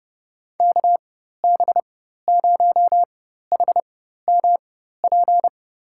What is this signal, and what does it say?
Morse code, heard as a keyed tone, about 32 wpm, spaced as for 14 wpm, 700 Hz. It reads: KB0HMP
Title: USA Call Signs - 6 Characters - 4x Character Spacing - 30wpm